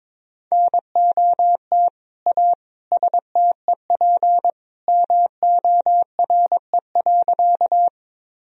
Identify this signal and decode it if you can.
Morse code, heard as a keyed tone, about 21 wpm, 700 Hz. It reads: NOT A STEP MORE.
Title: Sentences from Top 1000 Words - Review Part 2 - 22wpm